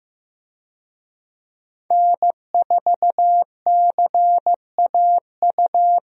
Morse code, heard as a keyed tone, about 15 wpm, 700 Hz. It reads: N4CAU
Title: USA Call Signs - 5 Characters - Repeated 2x - 15wpm